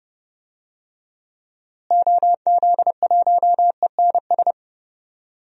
Morse code, heard as a keyed tone, about 30 wpm, 700 Hz. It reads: OZ1ENS
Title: DXCC-100 Call Signs - 6 Characters - Repeated 2x - 30wpm